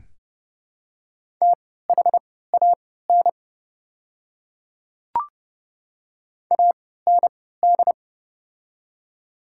Morse code, heard as a keyed tone, about 31 wpm, spaced as for 17 wpm, 700 Hz. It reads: THAN AND
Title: 3x Character Spacing - Top 100 Words - 30wpm